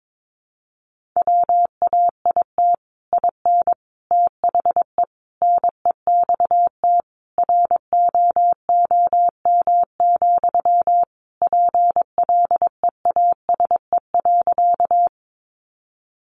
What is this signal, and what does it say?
Morse code, heard as a keyed tone, about 22 wpm, 700 Hz. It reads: WAIT IN THE NEXT ROOM, PLEASE.
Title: Sentences from Top 1500 Words 22wpm